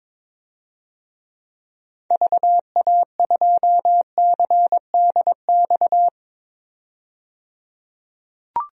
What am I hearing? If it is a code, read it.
Morse code, heard as a keyed tone, about 22 wpm, 700 Hz. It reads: VA2CDX
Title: DXCC-100 Call Signs - 6 Characters - 22wpm